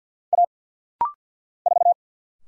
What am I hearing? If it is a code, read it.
Morse code, heard as a keyed tone, about 49 wpm, 700 Hz. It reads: A 4
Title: Single Letter-Number - Rapid-Fire 50wpm